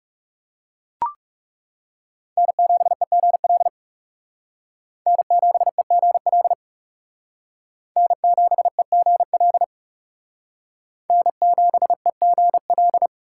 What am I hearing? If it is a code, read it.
Morse code, heard as a keyed tone, about 36 wpm, 700 Hz. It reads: N7EGL N7EGL N7EGL N7EGL
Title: Speed Racing - USA 1x3 Call Signs 30wpm